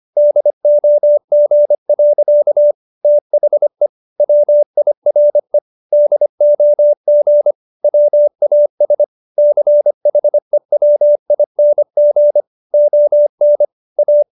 Morse code, heard as a keyed tone, about 25 wpm, 590 Hz. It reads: DOG. THE WIRE DOG WAS CHEWING ON A